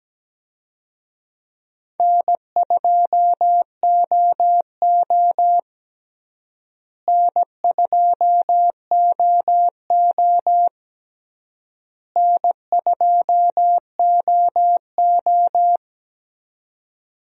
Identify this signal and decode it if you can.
Morse code, heard as a keyed tone, about 17 wpm, 700 Hz. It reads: N2OO N2OO N2OO
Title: USA Call Signs - 4 Characters - Repeated 3x - 17wpm